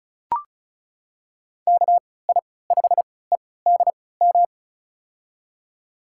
Morse code, heard as a keyed tone, about 36 wpm, spaced as for 20 wpm, 700 Hz. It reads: KI5EDM
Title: USA Call Signs - 6 Characters - 3x Character Spacing - 35wpm